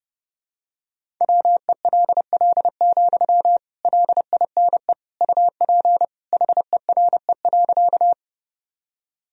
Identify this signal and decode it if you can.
Morse code, heard as a keyed tone, about 30 wpm, 700 Hz. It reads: WELL, LINE UP HERE.
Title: Sentences from Top 1000 Words - Review Part 1 - 30wpm